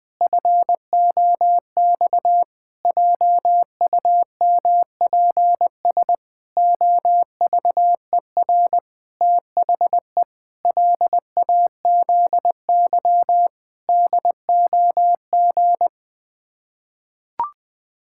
Morse code, heard as a keyed tone, about 20 wpm, 700 Hz. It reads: FOX JUMPS OVER THE LAZY DOG
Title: Sets of 8 Words - 20wpm